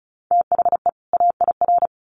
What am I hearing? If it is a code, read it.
Morse code, heard as a keyed tone, about 34 wpm, 700 Hz. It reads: THE AIR
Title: Sets of 2 Words 35wpm